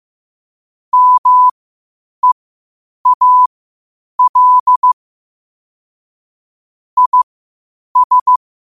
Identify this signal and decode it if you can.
Morse code, heard as a keyed tone, about 15 wpm, spaced as for 8 wpm, 1000 Hz. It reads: MEAL IS
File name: qotdfiles / 8wpm